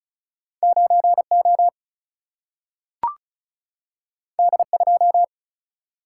Morse code, heard as a keyed tone, about 36 wpm, 700 Hz. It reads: 9O D2
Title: Single-Letter Single-Number 35wpm